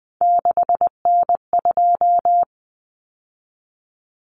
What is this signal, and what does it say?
Morse code, heard as a keyed tone, about 20 wpm, 700 Hz. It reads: 6N2
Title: Common Cut Numbers - 3-Digit 20wpm